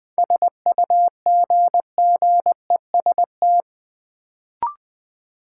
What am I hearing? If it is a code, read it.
Morse code, heard as a keyed tone, about 20 wpm, 700 Hz. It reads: SUGGEST
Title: Sentences from Top 1300 Words - New Word Drill Alt - 20wpm